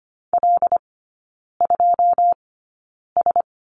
Morse code, heard as a keyed tone, about 25 wpm, spaced as for 9 wpm, 690 Hz. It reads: L2S
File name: Lesson 14e - Number 2 - Three Characters 25wpm